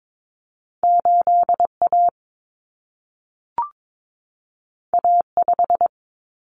Morse code, heard as a keyed tone, about 22 wpm, 700 Hz. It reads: 8A A5
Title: Common Cut Numbers - 2-Digit 22wpm